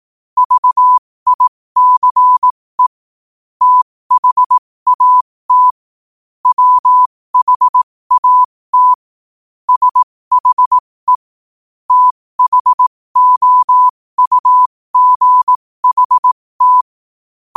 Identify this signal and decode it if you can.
Morse code, heard as a keyed tone, about 18 wpm, spaced as for 15 wpm, 1000 Hz. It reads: VICE THAT WHAT SHE THOUGHT
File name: qotdfiles / 15wpm